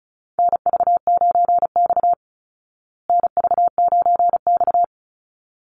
Morse code, heard as a keyed tone, about 34 wpm, 700 Hz. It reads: NV9X NV9X
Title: Call Signs Repeated 3x - 35wpm